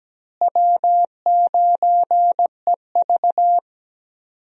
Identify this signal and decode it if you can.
Morse code, heard as a keyed tone, about 17 wpm, 700 Hz. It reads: W9EV
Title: USA Call Signs - 4 Characters - Repeated 2x - 17wpm